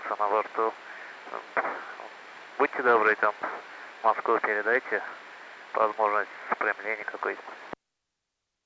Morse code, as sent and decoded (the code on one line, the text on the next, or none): none